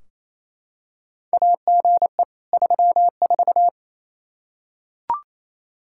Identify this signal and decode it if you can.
Morse code, heard as a keyed tone, about 28 wpm, 700 Hz. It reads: AGE 34
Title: Practice QSO - Exchange 3 - More Info - 28wpm